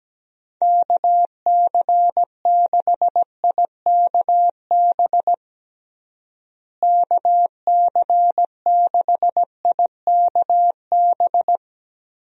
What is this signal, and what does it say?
Morse code, heard as a keyed tone, about 17 wpm, 700 Hz. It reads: KC6IKB KC6IKB
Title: USA Call Signs - 6 Characters - Repeated 3x - 17wpm